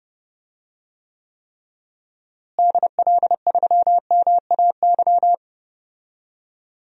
Morse code, heard as a keyed tone, about 30 wpm, 700 Hz. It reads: DL3MAY
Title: DXCC-100 Call Signs - 6 Characters - Repeated 3x - 30wpm